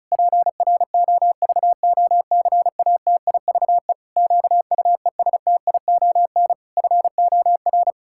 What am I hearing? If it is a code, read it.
Morse code, heard as a keyed tone, about 36 wpm, 700 Hz. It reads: PROVOCATIVE QUESTION FOR
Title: Pangram Challenge v2 - 35wpm